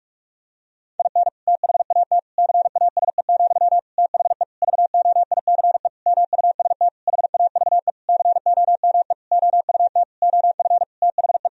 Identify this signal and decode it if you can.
Morse code, heard as a keyed tone, about 46 wpm, 700 Hz. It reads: IN THAT CASE, THE VOICE MUST HAVE COME OUT OF THE
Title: Sentences from Top 1000 Words - Review Part 2 - 45wpm